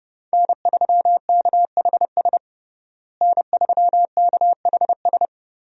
Morse code, heard as a keyed tone, about 30 wpm, 700 Hz. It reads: N3KHS N3KHS
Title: USA Call Signs - 5 Characters - Repeated 3x - 30wpm